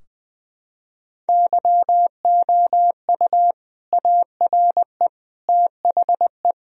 Morse code, heard as a keyed tone, about 20 wpm, 700 Hz. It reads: YOU ARE THE
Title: Sentences from Top 1000 Words - Review Part 4 - 20wpm